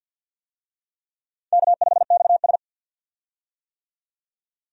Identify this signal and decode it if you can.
Morse code, heard as a keyed tone, about 51 wpm, 700 Hz. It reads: K5XS
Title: USA Call Signs - 4 Characters - 50wpm